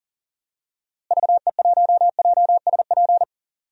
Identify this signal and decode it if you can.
Morse code, heard as a keyed tone, about 38 wpm, 700 Hz. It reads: VE1JSP
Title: DXCC-100 Call Signs - 6 Characters - Repeated 2x - 40wpm